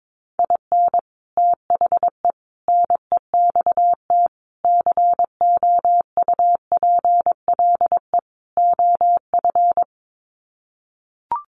Morse code, heard as a keyed tone, about 22 wpm, 700 Hz. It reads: IN THE NEXT COUPLE OF
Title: Sets of 5 Words Encore2 22wpm